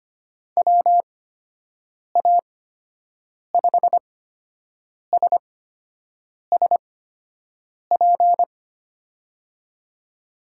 Morse code, heard as a keyed tone, about 25 wpm, spaced as for 7 wpm, 700 Hz. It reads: WA5SSP
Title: USA Call Signs - 6 Characters - 8x Character Spacing - 25wpm